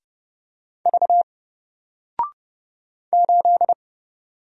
Morse code, heard as a keyed tone, about 30 wpm, 700 Hz. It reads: V 8
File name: Single-Letter-Number-30wpm